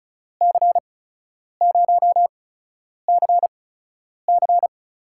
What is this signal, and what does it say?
Morse code, heard as a keyed tone, about 33 wpm, spaced as for 9 wpm, 700 Hz. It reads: C0CC
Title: USA Call Signs - 5 Characters - 8x Character Spacing - 35wpm